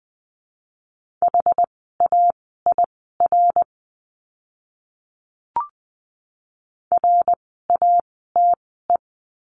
Morse code, heard as a keyed tone, about 20 wpm, spaced as for 14 wpm, 700 Hz. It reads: HAIR RATE
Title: 101 Lesson 17f - Letter u - Words 20wpm